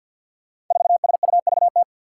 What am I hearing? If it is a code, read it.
Morse code, heard as a keyed tone, about 46 wpm, 700 Hz. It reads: 4SUVT
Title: Five Challenging Letter-Number Combinations 50wpm